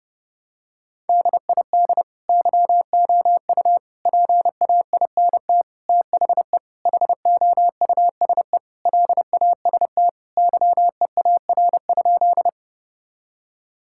Morse code, heard as a keyed tone, about 32 wpm, 700 Hz. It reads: DID YOU PAINT THE HOUSE LAST YEAR?
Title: Sentences from Top 1000 Words - Review Part 6 - 30wpm